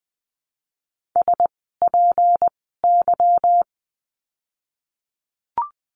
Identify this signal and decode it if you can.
Morse code, heard as a keyed tone, about 20 wpm, spaced as for 15 wpm, 700 Hz. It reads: SPY
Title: 153 Lesson 25f - Letter p - Words 20wpm